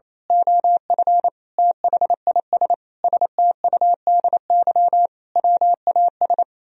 Morse code, heard as a keyed tone, about 26 wpm, 700 Hz. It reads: OF THIS STUDY WAS
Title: Speed-Racing - Sets of 5 Words 28wpm